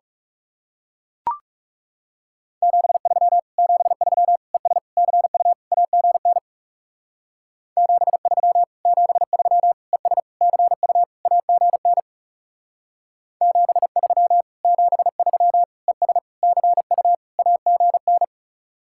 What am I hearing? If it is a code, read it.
Morse code, heard as a keyed tone, about 39 wpm, 700 Hz. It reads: 73 73 ES CU AGN 73 73 ES CU AGN 73 73 ES CU AGN